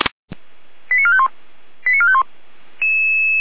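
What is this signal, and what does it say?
Five-tone selective call (ZVEI), heard as quick runs of five tones, tones of about 70 ms. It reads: 89451 89451
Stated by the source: ZVEI five-tone selective call received on an SDR.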